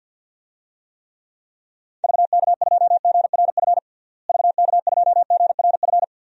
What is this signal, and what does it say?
Morse code, heard as a keyed tone, about 52 wpm, 700 Hz. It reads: VK2GRF VK2GRF
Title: DXCC-100 Call Signs - 6 Characters - Repeated 3x - 50wpm